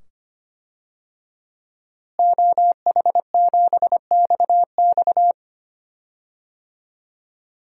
Morse code, heard as a keyed tone, about 25 wpm, 700 Hz. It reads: OH7XX
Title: DXCC-100 Call Signs - 5 Characters - Repeated 2x - 25wpm